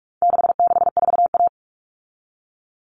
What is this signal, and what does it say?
Morse code, heard as a keyed tone, about 44 wpm, 700 Hz. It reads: 664A
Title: All Cut Numbers - 4-Digit 45wpm